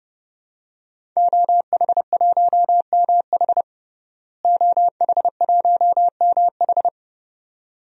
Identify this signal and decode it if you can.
Morse code, heard as a keyed tone, about 30 wpm, 700 Hz. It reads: OH1MH OH1MH
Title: DXCC-100 Call Signs - 5 Characters - Repeated 2x - 30wpm